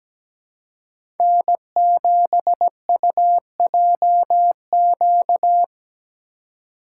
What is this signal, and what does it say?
Morse code, heard as a keyed tone, about 17 wpm, 700 Hz. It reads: N7UJQ
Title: USA Call Signs - 2x Character Spacing - 17wpm